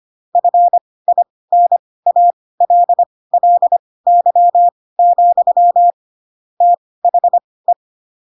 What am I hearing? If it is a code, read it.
Morse code, heard as a keyed tone, about 25 wpm, spaced as for 18 wpm, 700 Hz. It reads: FINALLY, THE
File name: Pinocchio ch3 12 25 ik1hge